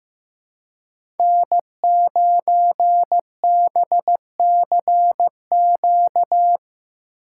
Morse code, heard as a keyed tone, about 15 wpm, 700 Hz. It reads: N9BCQ
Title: USA Call Signs - 5 Characters - 4x Character Spacing - 15wpm